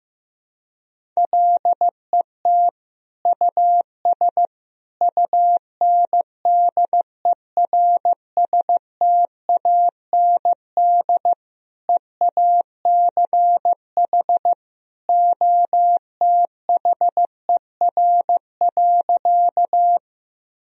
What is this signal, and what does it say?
Morse code, heard as a keyed tone, about 15 wpm, 700 Hz. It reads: LET US UNDERSTAND EACH OTHER.
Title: Sentences from Top 1000 Words - Review Part 1 - 15wpm